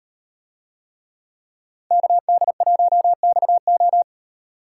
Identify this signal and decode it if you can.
Morse code, heard as a keyed tone, about 38 wpm, 700 Hz. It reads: KD1XO